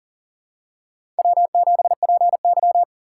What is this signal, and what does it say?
Morse code, heard as a keyed tone, about 40 wpm, 700 Hz. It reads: W7PY
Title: USA Call Signs - 4 Characters - 2x Character Spacing - 40wpm